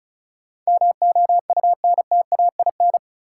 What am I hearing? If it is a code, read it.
Morse code, heard as a keyed tone, about 36 wpm, 700 Hz. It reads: MOUNTAIN
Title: Sentences from Top 1000 Words - Review Part 1 - New Word Drill - 35wpm